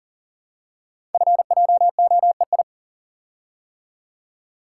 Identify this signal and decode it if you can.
Morse code, heard as a keyed tone, about 38 wpm, 700 Hz. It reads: FJOEI
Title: Code Groups - 5 Letters - 40wpm